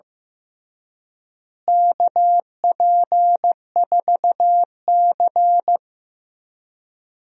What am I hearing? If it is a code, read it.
Morse code, heard as a keyed tone, about 15 wpm, 700 Hz. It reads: KP4C
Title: DXCC-100 Call Signs - 4 Characters - Repeated 3x - 15wpm